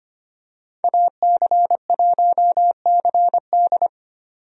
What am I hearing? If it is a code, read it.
Morse code, heard as a keyed tone, about 25 wpm, 700 Hz. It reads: AC1CD
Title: USA Call Signs - 5 Characters - 25wpm